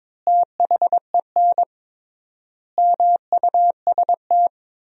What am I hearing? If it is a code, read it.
Morse code, heard as a keyed tone, about 22 wpm, 700 Hz. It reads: THEN MUST
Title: 3x Word-Spacing - Sentences from Top 500 Words - 22wpm